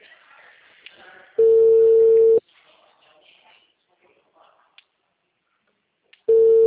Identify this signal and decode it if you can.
Ringback tone (China, 450 Hz).
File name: out- -2010- - - .16824